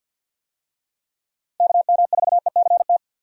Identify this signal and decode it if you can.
Morse code, heard as a keyed tone, about 52 wpm, 700 Hz. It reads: KM4ECT
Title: USA Call Signs - 6 Characters - 50wpm